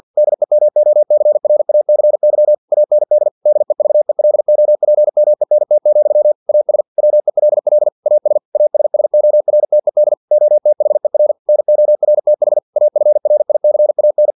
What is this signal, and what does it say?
Morse code, heard as a keyed tone, about 50 wpm, 600 Hz. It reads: DEMOCRACY AND DEVELOPMENT, AS WELL AS ASSORTED OTHER NORTH AFRICAN
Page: Never 49wpm